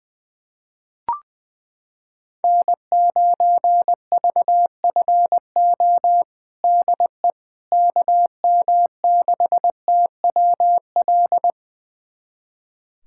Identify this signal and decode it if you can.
Morse code, heard as a keyed tone, about 21 wpm, 700 Hz. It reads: N9VFO DE KM6TWL
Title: USA Callsign DE Callsign 20wpm